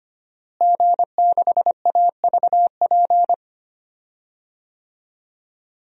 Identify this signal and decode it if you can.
Morse code, heard as a keyed tone, about 25 wpm, 700 Hz. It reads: G6AVP
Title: DXCC-100 Call Signs - 5 Characters - Repeated 3x - 25wpm